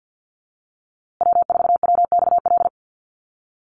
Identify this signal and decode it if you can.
Morse code, heard as a keyed tone, about 50 wpm, 700 Hz. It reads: W4WXL
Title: USA Call Signs Repeated 2x 50wpm